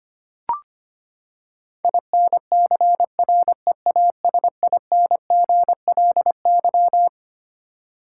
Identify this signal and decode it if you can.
Morse code, heard as a keyed tone, about 25 wpm, 700 Hz. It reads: INCREASINGLY
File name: Twelve Letter Words - 25wpm